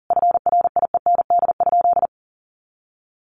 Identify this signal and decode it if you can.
Morse code, heard as a keyed tone, about 40 wpm, 700 Hz. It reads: FRIEND?
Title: Sentences from Top 300 Words 40wpm